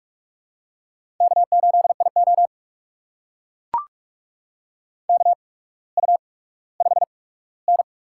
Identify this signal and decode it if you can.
Morse code, heard as a keyed tone, about 44 wpm, spaced as for 19 wpm, 700 Hz. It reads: K8IO KU5N
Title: USA Call Signs - 4 Characters - 8x Character Spacing - 45wpm